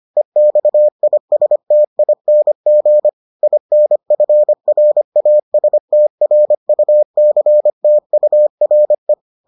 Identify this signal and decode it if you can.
Morse code, heard as a keyed tone, about 25 wpm, 600 Hz. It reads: EXISTING INFRASTRUCTURE